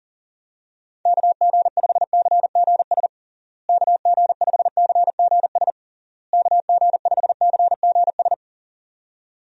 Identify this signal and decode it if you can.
Morse code, heard as a keyed tone, about 37 wpm, 700 Hz. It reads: KG5CGS KG5CGS KG5CGS
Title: USA Call Signs - 6 Characters - Repeated 3x - 40wpm